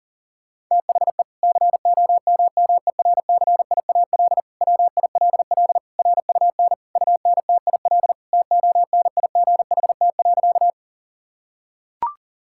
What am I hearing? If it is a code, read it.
Morse code, heard as a keyed tone, about 40 wpm, 700 Hz. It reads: THE COMMERCIAL WILL RUN UNTIL TONIGHT.
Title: Sentences from Top 1000 Words - Review Part 5 - 40wpm